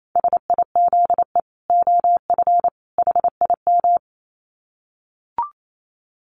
Morse code, heard as a keyed tone, about 28 wpm, 700 Hz. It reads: SIZE OF HIM
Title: Dit Challenge 28wpm